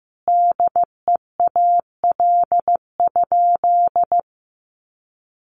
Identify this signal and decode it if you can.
Morse code, heard as a keyed tone, about 15 wpm, 700 Hz. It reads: DEAL?
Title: Sentences from Top 1400 Words 15wpm